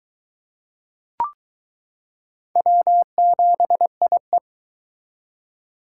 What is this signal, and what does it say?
Morse code, heard as a keyed tone, about 23 wpm, 700 Hz. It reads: W7IE